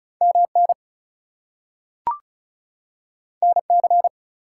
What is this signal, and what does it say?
Morse code, heard as a keyed tone, about 36 wpm, 700 Hz. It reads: MN NC
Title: US States and Canadian Provinces 35wpm